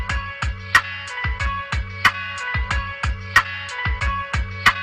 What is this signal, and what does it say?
Emergency Alert System (SAME) tones, heard as harsh buzzing bursts. It is heard from 1.0 to 4.0 s.